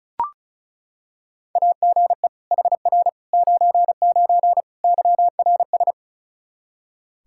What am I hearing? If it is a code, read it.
Morse code, heard as a keyed tone, about 36 wpm, 700 Hz. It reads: AGE HR 99 YRS
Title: Practice QSO - Exchange 3 - More Info - 35wpm